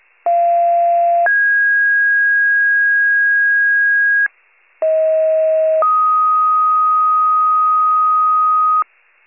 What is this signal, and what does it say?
Two-tone page, heard as a one-second tone, then a three-second tone, about 690 then 1650 Hz, about 640 then 1250 Hz, with noise. It is heard from 0.3 to 8.8 s.